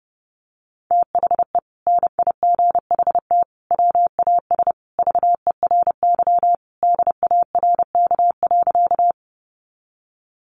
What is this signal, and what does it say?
Morse code, heard as a keyed tone, about 30 wpm, 700 Hz. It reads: THE NIGHT WAS VERY DARK.
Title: Sentences from Top 500 Words 30wpm